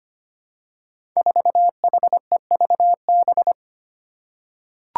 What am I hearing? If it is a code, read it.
Morse code, heard as a keyed tone, about 25 wpm, 700 Hz. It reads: 4HEVB
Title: Five Challenging Letter-Number Combinations 25wpm